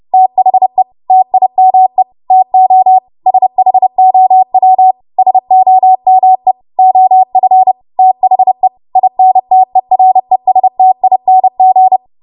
Morse code, heard as a keyed tone, about 30 wpm, 750 Hz. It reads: THE TIME TO SHOW SOME OF THE INTERESTING
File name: Morse / 30WPM